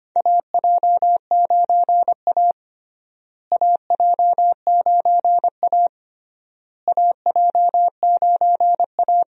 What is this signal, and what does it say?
Morse code, heard as a keyed tone, about 25 wpm, 700 Hz. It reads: AJ9A AJ9A AJ9A
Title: USA Call Signs - 4 Characters - Repeated 3x - 25wpm